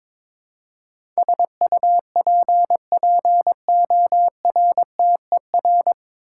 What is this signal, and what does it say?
Morse code, heard as a keyed tone, about 22 wpm, 700 Hz. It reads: SUPPORTER
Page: Morse Code Ninja practice 22wpm